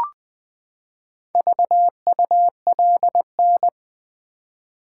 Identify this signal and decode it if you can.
Morse code, heard as a keyed tone, about 20 wpm, 700 Hz. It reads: VULN
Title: Build A Word - 4 Letters At A Time - 10-Letter Words - 20wpm